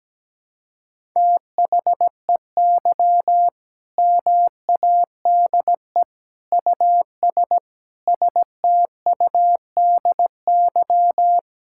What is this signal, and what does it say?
Morse code, heard as a keyed tone, about 17 wpm, 700 Hz. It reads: THEY MADE US STUDY
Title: Sentences from Top 1000 Words - Review Part 6 - 17wpm